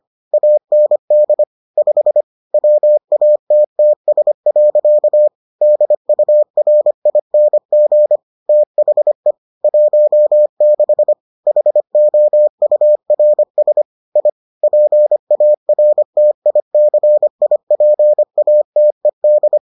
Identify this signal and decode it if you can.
Morse code, heard as a keyed tone, about 25 wpm, 600 Hz. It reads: AND 5 WATTS. DURING THE 16 HOURS I PARTICIPATED